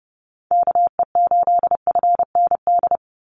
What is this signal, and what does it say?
Morse code, heard as a keyed tone, about 30 wpm, 700 Hz. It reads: KE8FND
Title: USA Call Signs 30wpm